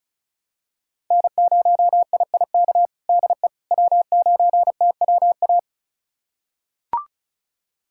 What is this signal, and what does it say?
Morse code, heard as a keyed tone, about 34 wpm, 700 Hz. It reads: N0IIK DE W9TWA
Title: USA Callsign DE Callsign - 5 Characters - 35wpm